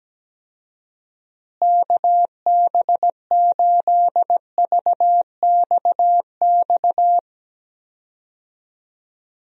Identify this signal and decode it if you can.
Morse code, heard as a keyed tone, about 17 wpm, 700 Hz. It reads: KB8VXX
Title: USA Call Signs - 6 Characters - Repeated 2x - 17wpm